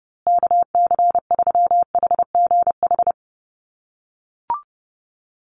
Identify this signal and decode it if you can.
Morse code, heard as a keyed tone, about 30 wpm, 700 Hz. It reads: KC3HGH
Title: USA Call Signs Repeated 2x 30wpm